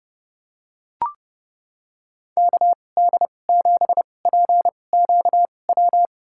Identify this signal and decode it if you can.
Morse code, heard as a keyed tone, about 30 wpm, spaced as for 22 wpm, 700 Hz. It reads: KD7PQW
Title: USA Call Signs - 6 Characters - 2x Character Spacing - 30wpm